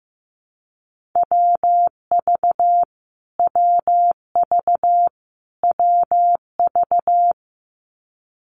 Morse code, heard as a keyed tone, about 15 wpm, 700 Hz. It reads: WV WV WV
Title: US States Repeated 3x 15wpm